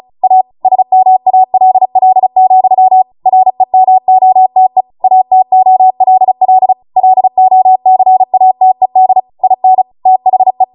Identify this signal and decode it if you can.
Morse code, heard as a keyed tone, about 36 wpm, 750 Hz. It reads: A SMALL, REMOTE ATOLL LOCATED IN THE